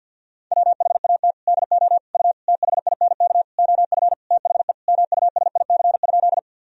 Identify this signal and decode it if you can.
Morse code, heard as a keyed tone, about 52 wpm, 700 Hz. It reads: WHAT DO U THINK OF THE MUSIC?
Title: Sentences from Top 300 Words and Abbreviations 50wpm